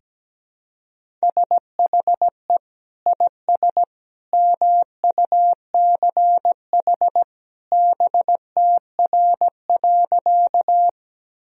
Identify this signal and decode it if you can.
Morse code, heard as a keyed tone, about 17 wpm, 700 Hz. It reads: SHE IS MUCH BTR.
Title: Sentences from Top 400 Words and Abbreviations 17wpm